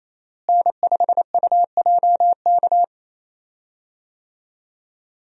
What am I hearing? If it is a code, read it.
Morse code, heard as a keyed tone, about 28 wpm, 700 Hz. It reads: N5UJK
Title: USA Call Signs - 5 Characters - Repeated 3x - 28wpm